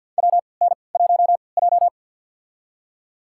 Morse code, heard as a keyed tone, about 46 wpm, spaced as for 27 wpm, 700 Hz. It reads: WN1J
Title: USA Call Signs - 3x Character Spacing - 50wpm